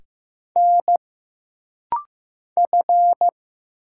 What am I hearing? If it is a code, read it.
Morse code, heard as a keyed tone, about 15 wpm, 700 Hz. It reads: N F
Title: Single Letters - Rapid-Fire 15wpm